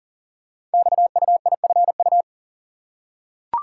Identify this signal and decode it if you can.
Morse code, heard as a keyed tone, about 40 wpm, 700 Hz. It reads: XUIFU
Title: Code Groups - 5 Letters - 40wpm